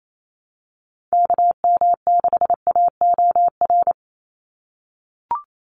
Morse code, heard as a keyed tone, about 28 wpm, 700 Hz. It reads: KM6AOR
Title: USA Call Signs 28wpm